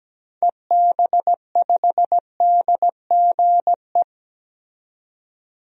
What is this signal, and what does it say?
Morse code, heard as a keyed tone, about 17 wpm, 700 Hz. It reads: EB5DGE
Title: DXCC-100 Call Signs - 6 Characters - 17wpm